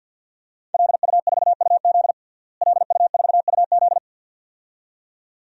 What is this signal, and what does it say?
Morse code, heard as a keyed tone, about 51 wpm, 700 Hz. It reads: LU4UZ LU4UZ
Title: DXCC-100 Call Signs - 5 Characters - Repeated 2x - 50wpm